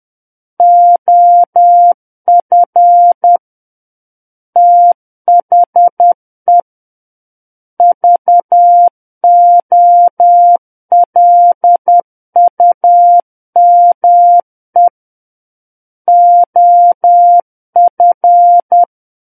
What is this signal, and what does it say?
Morse code, heard as a keyed tone, about 10 wpm, 700 Hz. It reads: OF THE VOLUME OF
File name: Book One / 01-Chapter 1 - 10 WPM